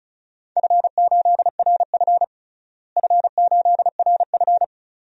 Morse code, heard as a keyed tone, about 34 wpm, 700 Hz. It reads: F8RF F8RF
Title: DXCC-100 Call Signs - 4 Characters - Repeated 2x - 35wpm